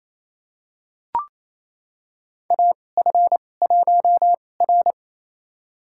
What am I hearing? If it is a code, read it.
Morse code, heard as a keyed tone, about 28 wpm, spaced as for 20 wpm, 700 Hz. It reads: AF1R
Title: USA Call Signs - 4 Characters - 2x Character Spacing - 28wpm